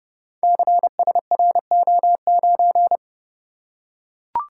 Morse code, heard as a keyed tone, about 30 wpm, 700 Hz. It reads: CSRO9
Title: Code Groups - 5 Letter:Numbers - 30wpm